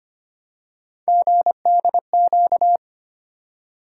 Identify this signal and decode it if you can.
Morse code, heard as a keyed tone, about 25 wpm, 700 Hz. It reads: GDQ
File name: Three Letter-Number Combinations - 25wpm